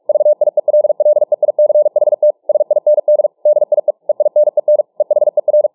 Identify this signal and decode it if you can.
Morse code, heard as a keyed tone, about 48 wpm, 600 Hz. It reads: VIELLEICHT SIND DIE EINEN EHER